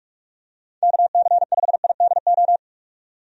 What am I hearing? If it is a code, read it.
Morse code, heard as a keyed tone, about 43 wpm, 700 Hz. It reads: KC5IDO
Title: USA Call Signs - 6 Characters - 45wpm